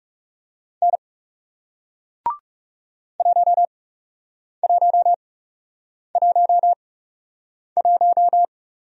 Morse code, heard as a keyed tone, about 40 wpm, spaced as for 8 wpm, 700 Hz. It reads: N 1111